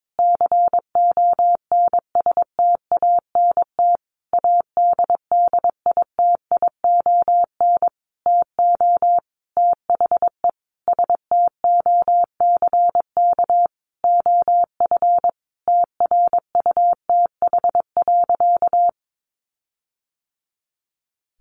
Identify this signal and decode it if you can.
Morse code, heard as a keyed tone, about 22 wpm, 700 Hz. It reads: CONSTANT ADDITION TO THE STOCK OF TRUTH.
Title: Sentences from Top 1500 Words 22wpm